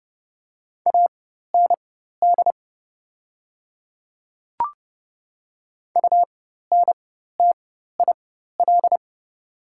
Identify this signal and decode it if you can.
Morse code, heard as a keyed tone, about 30 wpm, spaced as for 14 wpm, 700 Hz. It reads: AND UNTIL
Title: 4x Character Spacing - Top 500 Words - 30wpm